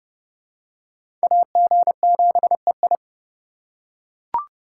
Morse code, heard as a keyed tone, about 27 wpm, 700 Hz. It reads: AG7EI